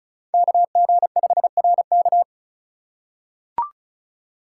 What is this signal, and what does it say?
Morse code, heard as a keyed tone, about 36 wpm, 700 Hz. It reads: KG5RK
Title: USA Call Signs - 5 Characters - 2x Character Spacing - 35wpm